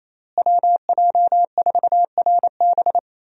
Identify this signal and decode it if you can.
Morse code, heard as a keyed tone, about 28 wpm, 700 Hz. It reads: WJ4RB
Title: USA Call Signs - 5 Characters - Repeated 3x - 28wpm